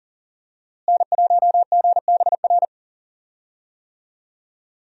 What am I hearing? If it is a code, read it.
Morse code, heard as a keyed tone, about 40 wpm, 700 Hz. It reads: N1GBR
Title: USA Call Signs - 5 Characters - Repeated 3x - 40wpm